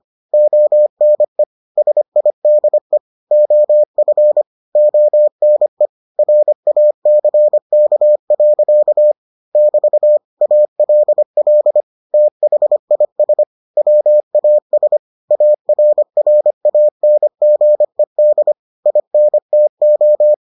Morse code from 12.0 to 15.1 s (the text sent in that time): THIS WAS